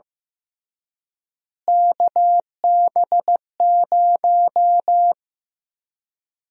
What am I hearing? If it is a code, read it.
Morse code, heard as a keyed tone, about 15 wpm, 700 Hz. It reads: KB0
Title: USA Call Sign Prefixes - Repeated 2x - 15wpm